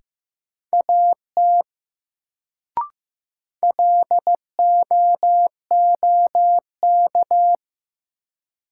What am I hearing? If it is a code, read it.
Morse code, heard as a keyed tone, about 15 wpm, 700 Hz. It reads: AT LOOK
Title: Sentences from Top 100 Words - New Word Drill - 15wpm